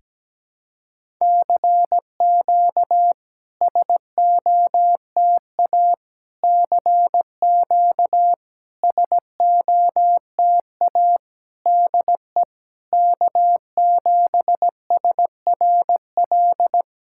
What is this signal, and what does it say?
Morse code, heard as a keyed tone, about 17 wpm, 690 Hz. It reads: CQ SOTA CQ SOTA DE K7SRL
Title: SOTA QSO Practice 17wpm